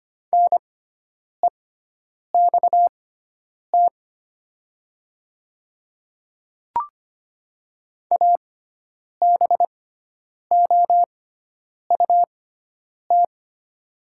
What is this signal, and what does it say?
Morse code, heard as a keyed tone, about 25 wpm, spaced as for 9 wpm, 700 Hz. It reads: NEXT ABOUT
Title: 6x Character Spacing - Top 300 Words - 25wpm